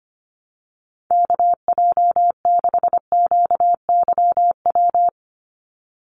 Morse code, heard as a keyed tone, about 25 wpm, 700 Hz. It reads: KJ6QYW
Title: USA Call Signs 25wpm